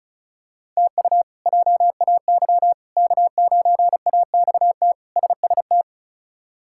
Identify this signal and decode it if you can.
Morse code, heard as a keyed tone, about 36 wpm, 700 Hz. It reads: TU JAY K9AXT SST
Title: K1USN SST QSO Practice - 35wpm